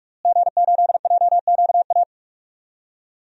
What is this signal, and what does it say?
Morse code, heard as a keyed tone, about 42 wpm, 700 Hz. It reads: G8JQA
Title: DXCC-100 Call Signs - 5 Characters - Repeated 3x - 45wpm